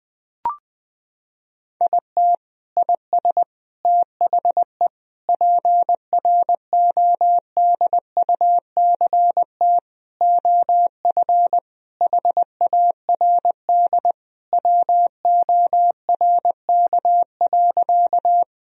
Morse code, heard as a keyed tone, about 21 wpm, 700 Hz. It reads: IT IS THE PRODUCT OF HARD WORK.
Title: Sentences from Top 1000 Words - Review Part 2 - 20wpm